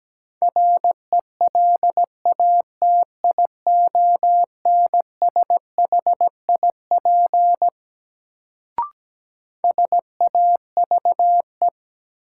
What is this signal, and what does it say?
Morse code, heard as a keyed tone, about 17 wpm, 700 Hz. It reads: RELATIONSHIP SAVE
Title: Sentences from Top 1100 Words - New Word Drill Alt - 17wpm